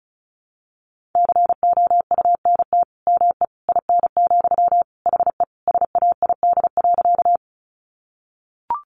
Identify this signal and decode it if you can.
Morse code, heard as a keyed tone, about 34 wpm, 700 Hz. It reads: COUNT ME IN, HE SAID.
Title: Sentences from Top 1500 Words 35wpm